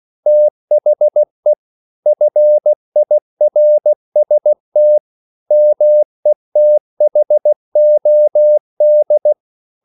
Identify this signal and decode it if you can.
Morse code, heard as a keyed tone, about 16 wpm, 600 Hz. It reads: THE FIRST METHOD
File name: CW / Practice / EnglishQuotations0216